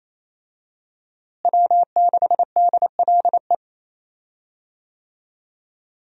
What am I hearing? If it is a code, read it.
Morse code, heard as a keyed tone, about 28 wpm, 700 Hz. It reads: W6DLE
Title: USA Call Signs - 5 Characters - Repeated 3x - 28wpm